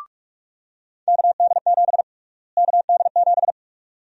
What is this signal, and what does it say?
Morse code, heard as a keyed tone, about 43 wpm, 700 Hz. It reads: KD7 KD7
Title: USA Call Sign Prefixes - Repeated 2x - 45wpm